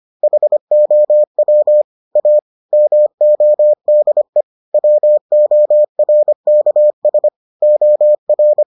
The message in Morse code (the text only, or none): HOW A MODE WORKS OR